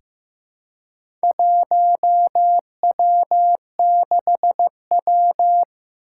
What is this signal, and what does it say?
Morse code, heard as a keyed tone, about 15 wpm, 700 Hz. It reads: 1W6W
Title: Four Challenging Letter-Number Combinations 15wpm